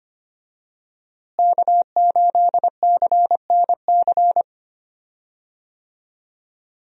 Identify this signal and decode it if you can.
Morse code, heard as a keyed tone, about 25 wpm, 700 Hz. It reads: K8CNC
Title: USA Call Signs - 5 Characters - Repeated 3x - 25wpm